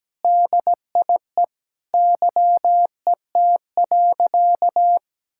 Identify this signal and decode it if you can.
Morse code, heard as a keyed tone, about 17 wpm, 700 Hz. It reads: DIE YET.
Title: Sentences from Top 1600 Words 17wpm